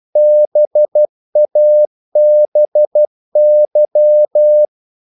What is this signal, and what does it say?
Morse code, heard as a keyed tone, about 12 wpm, 600 Hz. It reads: BABY
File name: Korte ord DK 12wpm